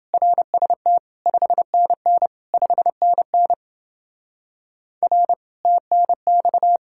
Morse code, heard as a keyed tone, about 30 wpm, 700 Hz. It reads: RST 5NN 5NN R TNX